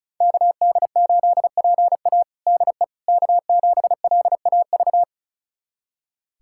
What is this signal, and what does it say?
Morse code, heard as a keyed tone, about 36 wpm, 700 Hz. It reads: KD8PA DE K7LAV
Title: USA Callsign DE Callsign - 5 Characters - 35wpm